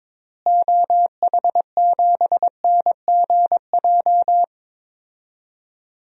Morse code, heard as a keyed tone, about 22 wpm, 700 Hz. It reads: OH7NGJ
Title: DXCC-100 Call Signs - 6 Characters - Repeated 3x - 22wpm